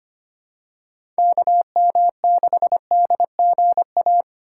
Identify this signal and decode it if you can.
Morse code, heard as a keyed tone, about 25 wpm, 700 Hz. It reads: KM6DGA
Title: USA Call Signs - 6 Characters - 2x Character Spacing - 25wpm